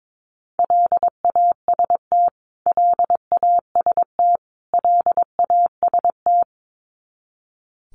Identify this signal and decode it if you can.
Morse code, heard as a keyed tone, about 22 wpm, 700 Hz. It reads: LAST LAST LAST
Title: Top 300 Words Repeated 22wpm